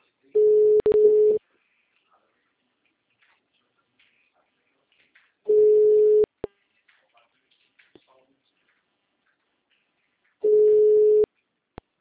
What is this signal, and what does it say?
Ringback tone (Europe, 425 Hz), with speech.